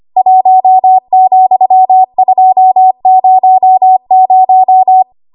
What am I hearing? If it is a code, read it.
Morse code, heard as a keyed tone, about 25 wpm, 750 Hz. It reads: 1,200
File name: Archive / 25 WPM / 25WPM